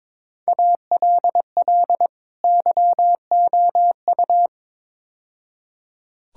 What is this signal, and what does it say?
Morse code, heard as a keyed tone, about 22 wpm, 700 Hz. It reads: ALL YOU
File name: practice / 22wpm / Build A Phrase - 5-Word Phrases - 22wpm